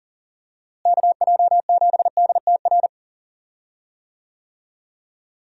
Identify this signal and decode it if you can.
Morse code, heard as a keyed tone, about 40 wpm, 700 Hz. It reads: KJ7DTR
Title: USA Call Signs - 6 Characters - Repeated 3x - 40wpm